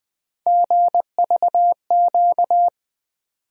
Morse code, heard as a keyed tone, about 21 wpm, 700 Hz. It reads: GVQ
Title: USA Call Sign Postfixes - 3 Letters 20wpm